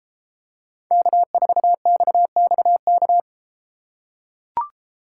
Morse code, heard as a keyed tone, about 33 wpm, 700 Hz. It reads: K4XXK